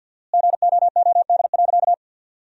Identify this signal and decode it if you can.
Morse code, heard as a keyed tone, about 50 wpm, 700 Hz. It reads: GOOD.
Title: 4x Word-Spacing - Sentences from Top 200 Words - 50wpm